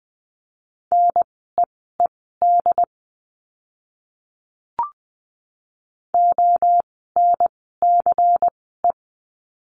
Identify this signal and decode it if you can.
Morse code, heard as a keyed tone, about 20 wpm, spaced as for 15 wpm, 700 Hz. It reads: NEED ONCE
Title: 108 Lesson 18f - Letter m - Words 20wpm